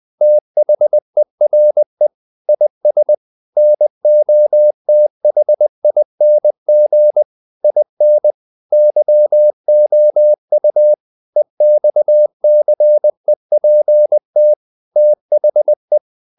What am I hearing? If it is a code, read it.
Morse code, heard as a keyed tone, about 20 wpm, 600 Hz. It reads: THERE IS NOTHING IN YOU EXCEPT THE
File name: CW / Practice / If20